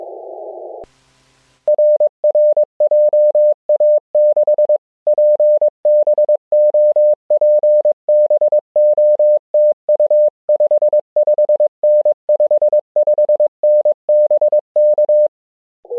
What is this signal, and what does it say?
Morse code, heard as a keyed tone, about 22 wpm, 600 Hz. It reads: RR JA6PBO PBO TU 55N 55N BK
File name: ja6pbo-g8bcg